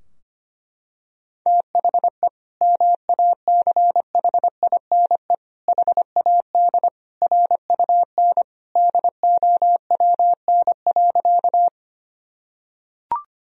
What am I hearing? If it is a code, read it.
Morse code, heard as a keyed tone, about 25 wpm, 700 Hz. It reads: THE MACHINE HAD RUN DOWN.
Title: Sentences from Top 1000 Words - Review Part 2 - 25wpm